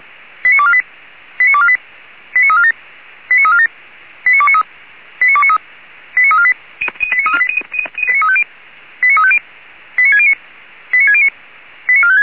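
Five-tone selective call (ZVEI), heard as quick runs of five tones, tones of about 70 ms. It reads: 89248 89248 89347 89347 89293 89293 89348 89350 89609 89609 89456
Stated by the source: ZVEI five-tone call received by SDR.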